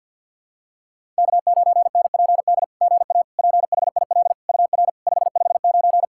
Morse code, heard as a keyed tone, about 48 wpm, 700 Hz. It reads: K9NPD GA PHIL UR 559
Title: SOTA QSO Practice 50wpm